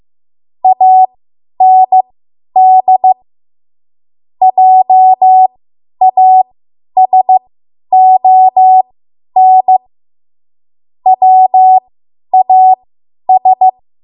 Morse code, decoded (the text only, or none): AND JASON WAS